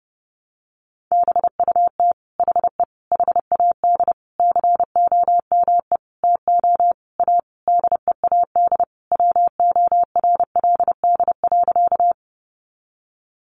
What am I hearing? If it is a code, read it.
Morse code, heard as a keyed tone, about 30 wpm, 700 Hz. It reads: BUT HE HAD COME TO A DEAD WORLD.
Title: Sentences from Top 1500 Words 30wpm